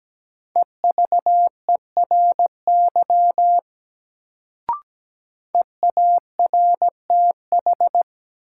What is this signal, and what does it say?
Morse code, heard as a keyed tone, about 17 wpm, 700 Hz. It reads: EVERY EARTH
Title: Sentences from Top 200 Words - New Word Drill - 17wpm